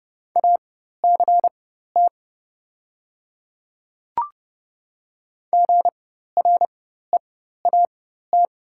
Morse code, heard as a keyed tone, about 30 wpm, spaced as for 14 wpm, 700 Hz. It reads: ACT GREAT
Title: 4x Character Spacing - Top 500 Words - 30wpm